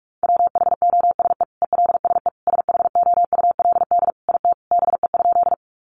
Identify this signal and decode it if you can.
Morse code, heard as a keyed tone, about 44 wpm, 700 Hz. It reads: WHOSE ELSE SHOULD IT BE?
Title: Sentences from Top 1200 Words 45wpm